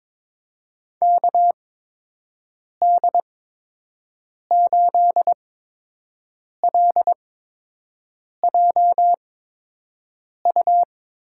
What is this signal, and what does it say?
Morse code, heard as a keyed tone, about 21 wpm, spaced as for 6 wpm, 700 Hz. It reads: KD8LJU
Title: USA Call Signs - 6 Characters - 8x Character Spacing - 22wpm